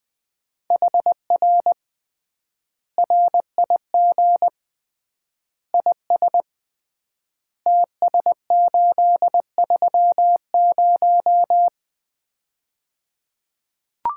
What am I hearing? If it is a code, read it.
Morse code, heard as a keyed tone, about 20 wpm, 700 Hz. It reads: HR RIG IS TS830
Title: Common QSO Phrases - 3x Word-Spacing - 20wpm